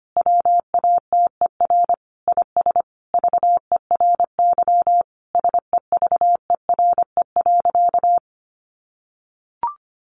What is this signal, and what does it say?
Morse code, heard as a keyed tone, about 25 wpm, 700 Hz. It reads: WATER IS VERY SEVERE.
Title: Sentences from Top 1500 Words 25wpm